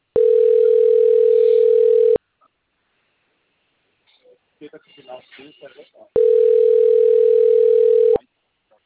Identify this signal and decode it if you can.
Ringback tone (North America), 440 + 480 Hz, with speech.
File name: rg-1600-unknown- - - .30801